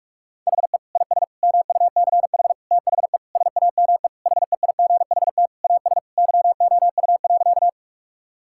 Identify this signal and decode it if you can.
Morse code, heard as a keyed tone, about 41 wpm, 700 Hz. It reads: HE IS MUCH THE SAME HEIGHT AS YOU.
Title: Sentences from Top 1600 Words 45wpm